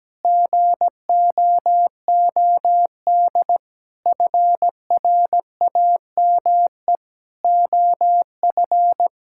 Morse code, heard as a keyed tone, about 17 wpm, 700 Hz. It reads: GOOD FRAME OF
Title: Sentences from Top 1000 Words - Review Part 6 - 17wpm